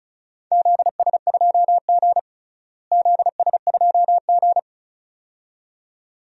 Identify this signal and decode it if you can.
Morse code, heard as a keyed tone, about 36 wpm, 700 Hz. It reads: ZS2G ZS2G
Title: DXCC-100 Call Signs - 4 Characters - Repeated 2x - 35wpm